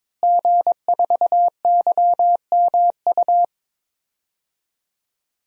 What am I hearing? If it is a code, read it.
Morse code, heard as a keyed tone, about 22 wpm, 700 Hz. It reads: G4YMU
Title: DXCC-100 Call Signs - 5 Characters - Repeated 2x - 22wpm